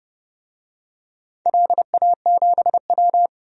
Morse code, heard as a keyed tone, about 30 wpm, 700 Hz. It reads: LA7W
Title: DXCC-100 Call Signs - 4 Characters - 30wpm